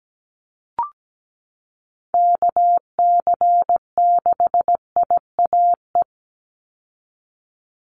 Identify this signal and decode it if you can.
Morse code, heard as a keyed tone, about 17 wpm, 700 Hz. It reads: KC6IAE
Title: USA Call Signs 17wpm